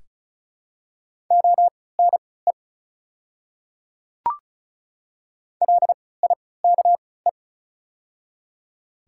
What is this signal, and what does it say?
Morse code, heard as a keyed tone, about 36 wpm, spaced as for 20 wpm, 700 Hz. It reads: ONE LIKE
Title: 3x Character Spacing - Top 200 Words - 35wpm